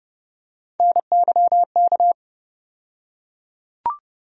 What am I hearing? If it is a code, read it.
Morse code, heard as a keyed tone, about 30 wpm, 700 Hz. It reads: NYK
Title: USA Call Sign Postfixes - 3 Letters 30wpm